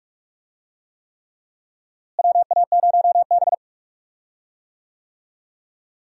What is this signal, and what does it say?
Morse code, heard as a keyed tone, about 41 wpm, 700 Hz. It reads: WA0B
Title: USA Call Signs - 4 Characters - Repeated 3x - 45wpm